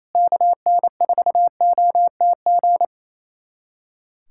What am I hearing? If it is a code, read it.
Morse code, heard as a keyed tone, about 28 wpm, 700 Hz. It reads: KN4OTG
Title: USA Callsign DE Callsign - 6 Characters - 28wpm